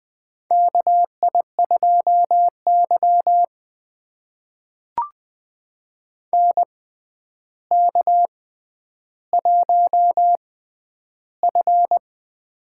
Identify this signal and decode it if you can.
Morse code, heard as a keyed tone, about 20 wpm, spaced as for 10 wpm, 700 Hz. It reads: KI2Y NK1F
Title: USA Call Signs - 4 Characters - 6x Character Spacing - 20wpm